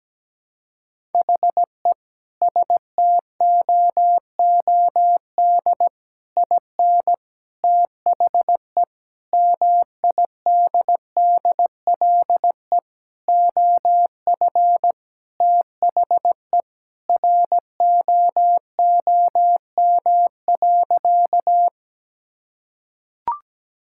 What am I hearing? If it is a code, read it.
Morse code, heard as a keyed tone, about 17 wpm, 700 Hz. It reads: HE STOOD IN THE MIDDLE OF THE ROOM.
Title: Sentences from Top 1000 Words - Review Part 6 - 17wpm